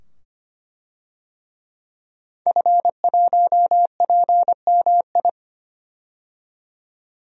Morse code, heard as a keyed tone, about 25 wpm, 700 Hz. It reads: F1PMI
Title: DXCC-100 Call Signs - 5 Characters - 25wpm